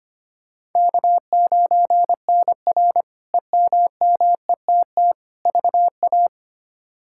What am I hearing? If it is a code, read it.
Morse code, heard as a keyed tone, about 25 wpm, 700 Hz. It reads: K9NR EMMETT VA
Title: K1USN SST QSO Practice - 25wpm